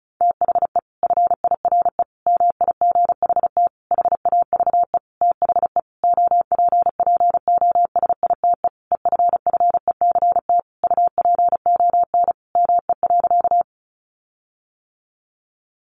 Morse code, heard as a keyed tone, about 34 wpm, 700 Hz. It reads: THE FIRE MIGHT HAVE THE OPPOSITE EFFECT UPON ME.
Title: Sentences from Top 1300 Words 35wpm